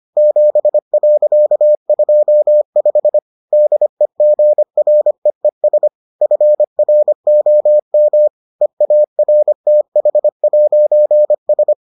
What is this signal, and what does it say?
Morse code, heard as a keyed tone, about 25 wpm, 600 Hz. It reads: 7.25 DEGREES FROM EARTH'S